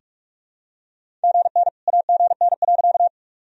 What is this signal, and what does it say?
Morse code, heard as a keyed tone, about 46 wpm, 700 Hz. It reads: GN AGN.
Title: Sentences from Top 300 Words and Abbreviations 45wpm